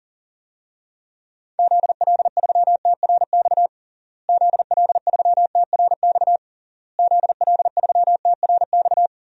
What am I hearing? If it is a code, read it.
Morse code, heard as a keyed tone, about 40 wpm, 700 Hz. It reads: ZL3TRX ZL3TRX ZL3TRX
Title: DXCC-100 Call Signs - 6 Characters - Repeated 3x - 40wpm